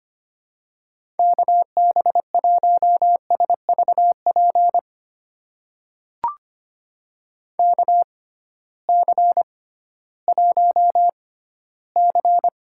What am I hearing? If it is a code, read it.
Morse code, heard as a keyed tone, about 25 wpm, spaced as for 15 wpm, 700 Hz. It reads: KB1SVP KC1C
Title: USA Call Signs - 6 Characters - 6x Character Spacing - 25wpm